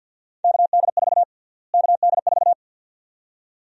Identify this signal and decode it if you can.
Morse code, heard as a keyed tone, about 50 wpm, 700 Hz. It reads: KD4 KD4
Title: USA Call Sign Prefixes - Repeated 2x - 50wpm